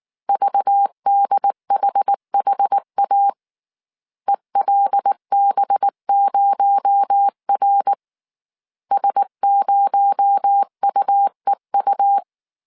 Morse code, heard as a keyed tone, about 19 wpm, 800 Hz. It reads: VDHHA ELB0R S0UEU
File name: CW / DET / 1 6 1500C S C 18wpm 2356